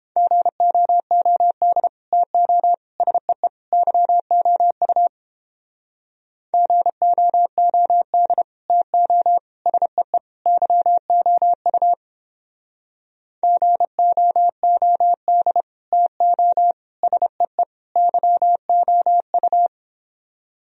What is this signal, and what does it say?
Morse code, heard as a keyed tone, about 29 wpm, 700 Hz. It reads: GOOD TO SEE YOU GOOD TO SEE YOU GOOD TO SEE YOU